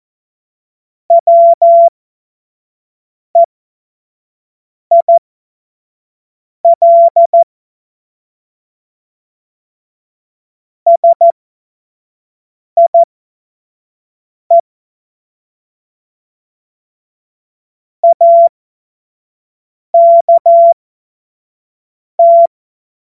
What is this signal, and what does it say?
Morse code, heard as a keyed tone, about 14 wpm, spaced as for 5 wpm, 680 Hz. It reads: WEIL SIE AKT
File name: Witz-15-Juli-2025-25-F70